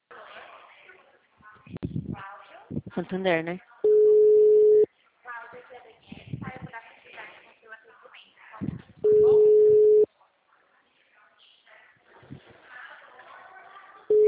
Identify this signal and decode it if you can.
Ringback tone (Europe, 425 Hz), with speech.
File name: out- -2048- - - .25061